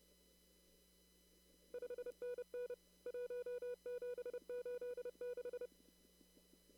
Morse code, heard as a keyed tone, about 28 wpm, 490 Hz. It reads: 5NN 1786
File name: CW / kt5j wpx cw-op-k5pi